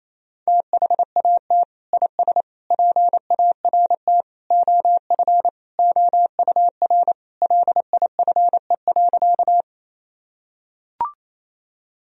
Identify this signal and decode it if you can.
Morse code, heard as a keyed tone, about 28 wpm, 700 Hz. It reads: THAT IS PART OF OUR LIFE.
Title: Sentences from Top 1000 Words - Review Part 1 - 28wpm